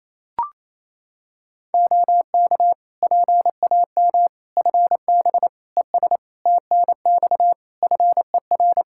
Morse code, heard as a keyed tone, about 27 wpm, 700 Hz. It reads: OK PAM FB ES TNX FER
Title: K1USN SST QSO Practice 28wpm